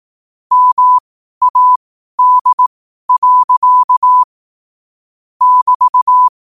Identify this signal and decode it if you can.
Morse code, heard as a keyed tone, about 18 wpm, spaced as for 12 wpm, 1000 Hz. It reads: MAD. <BT>
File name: qotdfiles / 12wpm